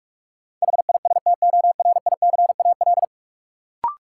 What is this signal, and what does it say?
Morse code, heard as a keyed tone, about 46 wpm, 700 Hz. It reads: HISTORICAL
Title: Sentences from Top 600 Words - New Word Drill - 45wpm